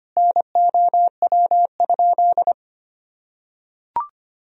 Morse code, heard as a keyed tone, about 25 wpm, 700 Hz. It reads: NOW?
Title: Sentences from Top 100 Words and Abbreviations 25wpm